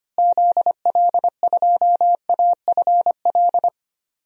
Morse code, heard as a keyed tone, about 25 wpm, 700 Hz. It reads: ZL2AFL
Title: DXCC-100 Call Signs - 6 Characters - Repeated 2x - 25wpm